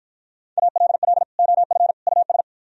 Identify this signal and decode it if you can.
Morse code, heard as a keyed tone, about 52 wpm, 700 Hz. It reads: ALL OF US